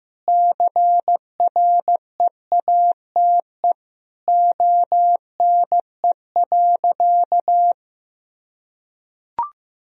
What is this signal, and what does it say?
Morse code, heard as a keyed tone, about 15 wpm, 700 Hz. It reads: CREATE ONE.
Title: Sentences from Top 1000 Words - Review Part 3 - 15wpm